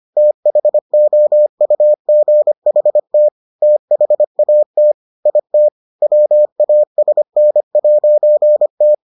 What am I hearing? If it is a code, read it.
Morse code, heard as a keyed tone, about 25 wpm, 600 Hz. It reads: THOUGHT THAT IT WASN'T